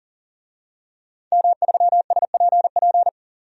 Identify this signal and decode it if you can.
Morse code, heard as a keyed tone, about 40 wpm, 700 Hz. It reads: M3SPP
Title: DXCC-100 Call Signs - 5 Characters - 40wpm